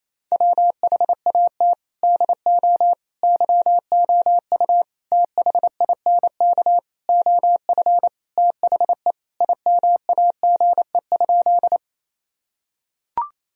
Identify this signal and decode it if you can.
Morse code, heard as a keyed tone, about 28 wpm, 700 Hz. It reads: WHAT DO YOU THINK OF THE IMAGE?
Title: Sentences from Top 1000 Words - Review Part 4 - 28wpm